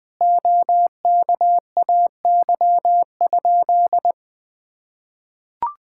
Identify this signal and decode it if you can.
Morse code, heard as a keyed tone, about 21 wpm, 700 Hz. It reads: OKAY?
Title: Sentences with 2-Word Phrases - 20wpm